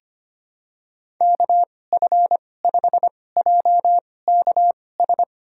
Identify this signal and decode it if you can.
Morse code, heard as a keyed tone, about 25 wpm, spaced as for 18 wpm, 700 Hz. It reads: KF5JKS
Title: USA Call Signs - 2x Character Spacing - 25wpm